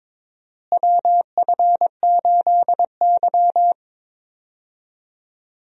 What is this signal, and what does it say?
Morse code, heard as a keyed tone, about 22 wpm, 700 Hz. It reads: WF8Y
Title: USA Call Signs - 4 Characters - Repeated 2x - 22wpm